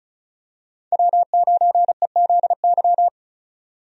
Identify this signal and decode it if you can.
Morse code, heard as a keyed tone, about 34 wpm, 700 Hz. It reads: W9EZY
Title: USA Call Signs - 5 Characters - 35wpm